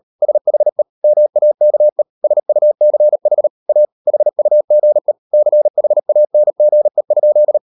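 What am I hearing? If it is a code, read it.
Morse code, heard as a keyed tone, about 38 wpm, 600 Hz. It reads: SHE MAKE SUCH A HUGE CHANGE?
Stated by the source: Morse code practice at 38 WPM, sending Ken Follett's novel Never in English, from CW med Gnister.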